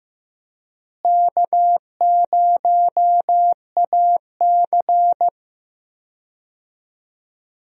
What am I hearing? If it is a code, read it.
Morse code, heard as a keyed tone, about 15 wpm, 700 Hz. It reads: K0AC
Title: USA Call Signs - 4 Characters - Repeated 2x - 15wpm